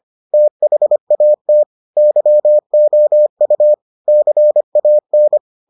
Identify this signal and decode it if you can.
Morse code, heard as a keyed tone, about 25 wpm, 600 Hz. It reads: THAT YOU CAN